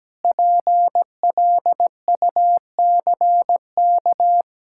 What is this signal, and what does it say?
Morse code, heard as a keyed tone, about 17 wpm, 700 Hz. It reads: PLUCK
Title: Pangram Challenge v2 - 17wpm